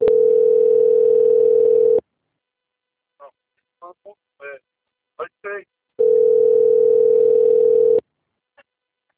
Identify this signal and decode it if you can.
Ringback tone (North America), 440 + 480 Hz, with speech.
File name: out- - - - - .5989